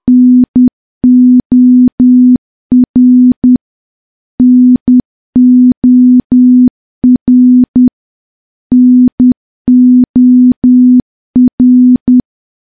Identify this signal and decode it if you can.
Morse code, heard as a keyed tone, about 10 wpm, 250 Hz. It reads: NOR NOR NOR
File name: morse / morse-training-10WPM